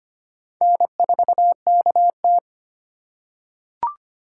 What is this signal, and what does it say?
Morse code, heard as a keyed tone, about 25 wpm, 700 Hz. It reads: N4KT
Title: USA Call Signs - 4 Characters - 4x Character Spacing - 25wpm